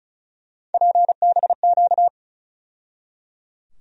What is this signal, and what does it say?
Morse code, heard as a keyed tone, about 34 wpm, 700 Hz. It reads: PBQ
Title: USA Call Sign Postfixes - 3 Letters 35wpm